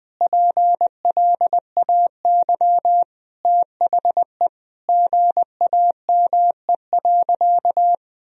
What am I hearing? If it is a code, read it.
Morse code, heard as a keyed tone, about 21 wpm, 700 Hz. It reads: PLAY THE GAME.
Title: Sentences from Top 1000 Words - Review Part 3 - 20wpm